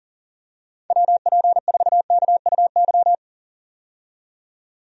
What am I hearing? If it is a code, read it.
Morse code, heard as a keyed tone, about 40 wpm, 700 Hz. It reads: WP4KUY
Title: USA Call Signs - 6 Characters - 40wpm